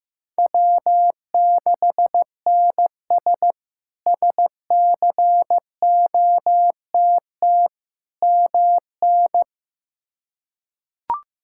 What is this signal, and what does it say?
Morse code, heard as a keyed tone, about 15 wpm, 700 Hz. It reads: W6NS SCOTT MN
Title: K1USN SST QSO Practice - 15wpm